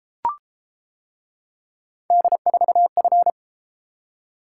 Morse code, heard as a keyed tone, about 31 wpm, 700 Hz. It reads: D4F